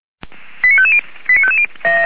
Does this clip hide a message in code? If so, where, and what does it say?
Yes: five-tone selective call (ZVEI) from 0.6 to 1.7 s, heard as quick runs of five tones, tones of about 70 ms. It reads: 89550 89550